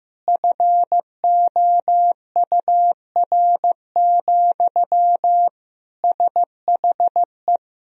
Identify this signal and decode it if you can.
Morse code, heard as a keyed tone, about 15 wpm, 700 Hz. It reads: FOUR, SHE
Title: Sentences from Top 1000 Words - Review Part 1 - 15wpm